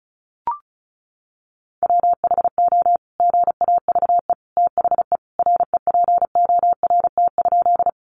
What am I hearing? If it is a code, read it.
Morse code, heard as a keyed tone, about 34 wpm, 700 Hz. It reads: WHO GAVE THE REPORT?
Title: Sentences from Top 900 Words 35wpm